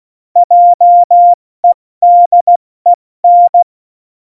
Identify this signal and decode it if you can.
Morse code, heard as a keyed tone, about 16 wpm, 700 Hz. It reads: JEDEN
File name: Morsekurs / Witze / Witz 23. Oktober 2018, 70 F80